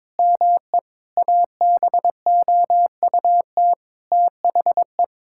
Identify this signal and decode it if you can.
Morse code, heard as a keyed tone, about 22 wpm, 700 Hz. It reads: ME ABOUT THE
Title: Sentences from Top 1000 Words - Review Part 5 - 22wpm